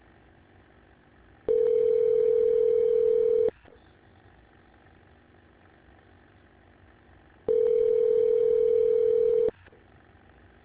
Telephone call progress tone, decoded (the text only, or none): ringback tone